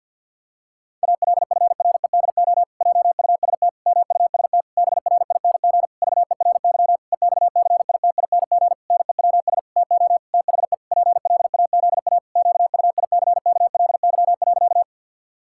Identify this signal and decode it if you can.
Morse code, heard as a keyed tone, about 50 wpm, 700 Hz. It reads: ALFREDO JUST MUST BRING VERY EXCITING NEWS TO THE PLAZA QUICKLY.
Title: Pangram Challenge v2 - 50wpm